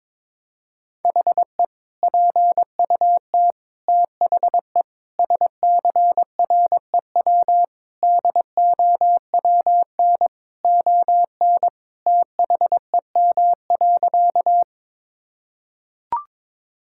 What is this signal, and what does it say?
Morse code, heard as a keyed tone, about 22 wpm, 700 Hz. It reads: HE PUT THE SCREW DOWN ON THEM.
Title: Sentences from Top 1600 Words 22wpm